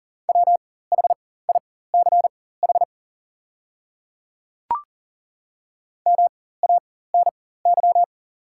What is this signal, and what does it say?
Morse code, heard as a keyed tone, about 40 wpm, spaced as for 19 wpm, 690 Hz. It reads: WHICH MANY
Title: 4x Character Spacing - Top 100 Words - 40wpm